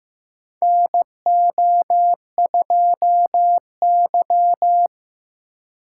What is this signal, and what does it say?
Morse code, heard as a keyed tone, about 15 wpm, 700 Hz. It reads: NO2Y
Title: USA Call Signs - 4x Character Spacing - 15wpm